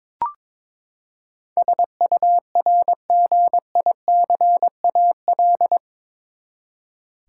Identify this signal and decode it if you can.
Morse code, heard as a keyed tone, about 22 wpm, 700 Hz. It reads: SURGICAL
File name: practice / 22wpm / Eight Letter Words - 22wpm